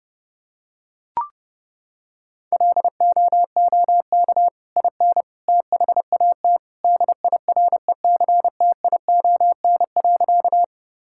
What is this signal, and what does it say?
Morse code, heard as a keyed tone, about 30 wpm, 700 Hz. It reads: LOOK IN THAT DIRECTION.
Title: Sentences from Top 1000 Words - Review Part 5 - 30wpm